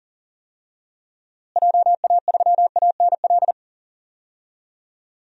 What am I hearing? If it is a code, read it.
Morse code, heard as a keyed tone, about 40 wpm, 700 Hz. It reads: JA3ANL
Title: DXCC-100 Call Signs - 6 Characters - Repeated 2x - 40wpm